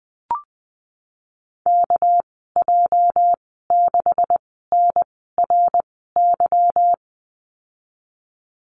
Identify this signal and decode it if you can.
Morse code, heard as a keyed tone, about 20 wpm, spaced as for 14 wpm, 700 Hz. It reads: KJ6NRY
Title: 207 Lesson 33g - Letter j - Call Signs 20wpm